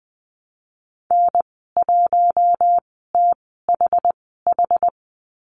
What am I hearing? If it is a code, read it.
Morse code, heard as a keyed tone, about 20 wpm, spaced as for 14 wpm, 700 Hz. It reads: N1THH
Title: 069 Lesson 12g - Letter d - Call Signs 20wpm